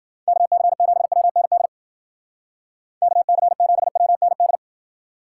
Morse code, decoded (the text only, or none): KC7WND KC7WND